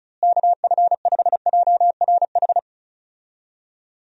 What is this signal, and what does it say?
Morse code, heard as a keyed tone, about 36 wpm, 700 Hz. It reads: KF5JRH
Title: USA Call Signs - 6 Characters - Repeated 2x - 35wpm